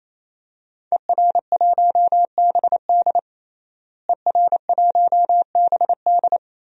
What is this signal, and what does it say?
Morse code, heard as a keyed tone, about 28 wpm, 700 Hz. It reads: ER1BD ER1BD
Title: DXCC-100 Call Signs - 5 Characters - Repeated 3x - 28wpm